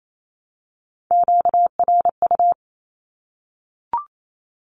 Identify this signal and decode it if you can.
Morse code, heard as a keyed tone, about 28 wpm, 700 Hz. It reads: QRU
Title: Q-Codes Repeated 3x 28wpm